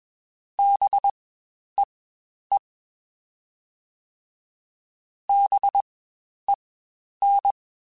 Morse code, heard as a keyed tone, about 21 wpm, spaced as for 10 wpm, 800 Hz. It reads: BEE BEN
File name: Spaced Alphas / Spaced Alphas mp3s / Bs sp @20